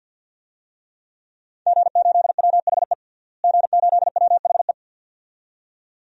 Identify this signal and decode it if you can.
Morse code, heard as a keyed tone, about 51 wpm, 700 Hz. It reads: G8WHE G8WHE
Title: DXCC-100 Call Signs - 5 Characters - Repeated 2x - 50wpm